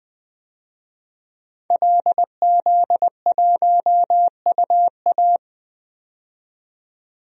Morse code, heard as a keyed tone, about 21 wpm, 700 Hz. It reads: LZ1UA
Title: DXCC-100 Call Signs - 5 Characters - 20wpm